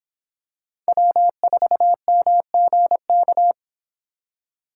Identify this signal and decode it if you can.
Morse code, heard as a keyed tone, about 25 wpm, 700 Hz. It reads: W4MGK